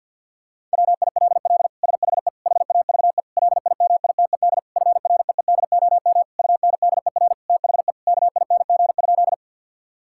Morse code, heard as a keyed tone, about 50 wpm, 700 Hz. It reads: WILL SHE HAVE LIMITED FREEDOM UNDER THE KING?
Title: Sentences from Top 1000 Words - Review Part 6 - 50wpm